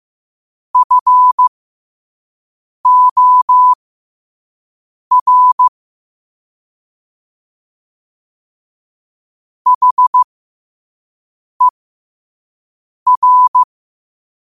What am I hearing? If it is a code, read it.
Morse code, heard as a keyed tone, about 15 wpm, spaced as for 5 wpm, 1000 Hz. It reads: FOR HER
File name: qotdfiles / 5wpm